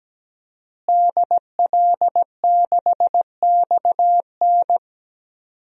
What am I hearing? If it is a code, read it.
Morse code, heard as a keyed tone, about 17 wpm, 700 Hz. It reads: DL6XN
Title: DXCC-100 Call Signs - 5 Characters - Repeated 3x - 17wpm